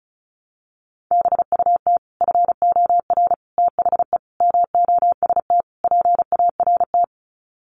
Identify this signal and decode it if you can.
Morse code, heard as a keyed tone, about 34 wpm, 700 Hz. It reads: BUT FOR THE MOST PART
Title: Sets of 5 Words Encore 35wpm